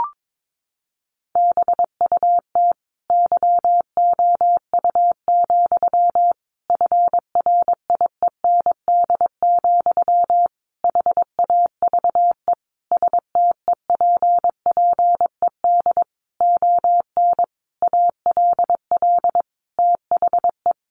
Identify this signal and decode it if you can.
Morse code, heard as a keyed tone, about 22 wpm, 700 Hz. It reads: BUT YOU, FRIEND, HAVE STEPPED ON ALL THE
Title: A Sound Of Thunder 22wpm